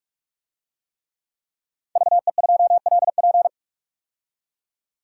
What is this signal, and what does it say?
Morse code, heard as a keyed tone, about 46 wpm, 700 Hz. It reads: VE2LP
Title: DXCC-100 Call Signs - 5 Characters - 45wpm